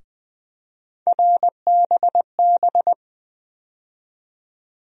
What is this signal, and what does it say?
Morse code, heard as a keyed tone, about 21 wpm, 700 Hz. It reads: RBB
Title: USA Call Sign Postfixes - 3 Letters 20wpm